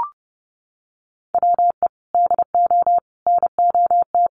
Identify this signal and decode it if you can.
Morse code, heard as a keyed tone, about 30 wpm, 700 Hz. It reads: WE DO NOT
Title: Sets of 5 Words 30wpm